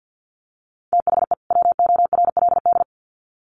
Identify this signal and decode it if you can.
Morse code, heard as a keyed tone, about 50 wpm, 700 Hz. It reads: THE WORLD
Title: Sets of 2 Words 50wpm